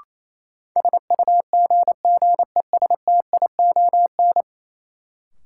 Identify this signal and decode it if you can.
Morse code, heard as a keyed tone, about 28 wpm, 700 Hz. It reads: SUGGESTION
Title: Sentences from Top 1600 Words - New Word Drill Alt - 28wpm